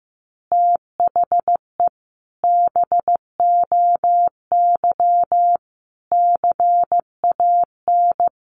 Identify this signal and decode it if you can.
Morse code, heard as a keyed tone, about 15 wpm, 700 Hz. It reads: THE BOY CAN
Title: Additional Easy Phrases 15wpm